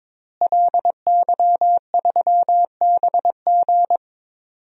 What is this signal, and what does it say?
Morse code, heard as a keyed tone, about 22 wpm, 700 Hz. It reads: LY3BG
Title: DXCC-100 Call Signs - 5 Characters - Repeated 2x - 22wpm